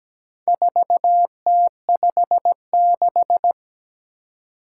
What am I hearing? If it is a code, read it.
Morse code, heard as a keyed tone, about 17 wpm, 700 Hz. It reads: 4T56
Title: Four Challenging Letter-Number Combinations 17wpm